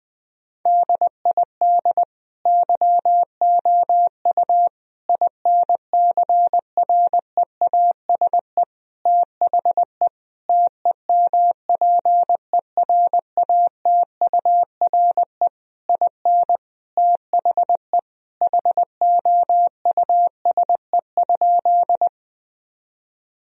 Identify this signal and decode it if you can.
Morse code, heard as a keyed tone, about 20 wpm, 700 Hz. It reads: DID YOU INCREASE THE TEMPERATURE IN THE HOUSE?
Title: Sentences from Top 1000 Words - Review Part 4 - 20wpm